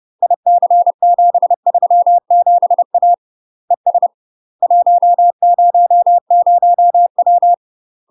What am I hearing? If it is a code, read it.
Morse code, heard as a keyed tone, about 30 wpm, 700 Hz. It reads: IC737A ES 100W
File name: UK MASTER QSOs TRAINING / 30WPM / UK MASTER QSOs 0021